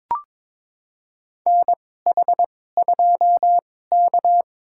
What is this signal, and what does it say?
Morse code, heard as a keyed tone, about 23 wpm, spaced as for 16 wpm, 700 Hz. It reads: NH2K
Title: USA Call Signs - 2x Character Spacing - 22wpm